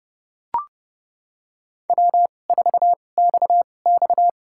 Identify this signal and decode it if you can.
Morse code, heard as a keyed tone, about 30 wpm, spaced as for 22 wpm, 700 Hz. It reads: W4XX
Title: USA Call Signs - 4 Characters - 2x Character Spacing - 30wpm